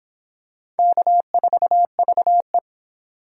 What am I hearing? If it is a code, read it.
Morse code, heard as a keyed tone, about 24 wpm, 700 Hz. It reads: K4VE